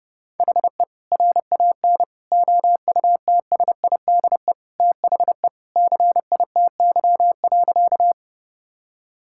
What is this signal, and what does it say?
Morse code, heard as a keyed tone, about 32 wpm, 700 Hz. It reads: HE RAN OUTSIDE THE CITY.
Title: Sentences from Top 1000 Words - Review Part 6 - 30wpm